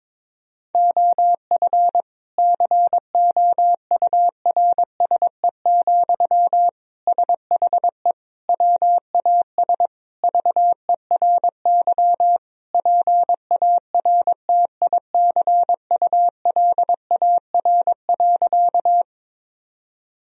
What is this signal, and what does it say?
Morse code, heard as a keyed tone, about 22 wpm, 700 Hz. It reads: OF COURSE, SHE WAS VERY PARTICULAR.
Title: Sentences from Top 1000 Words - Review Part 4 - 22wpm